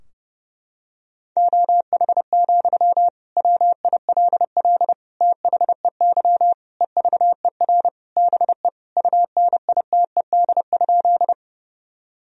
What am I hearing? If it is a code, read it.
Morse code, heard as a keyed tone, about 30 wpm, 700 Hz. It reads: OH, WILL THEY EVER BE UNITED?
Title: Sentences from Top 1000 Words - Review Part 4 - 30wpm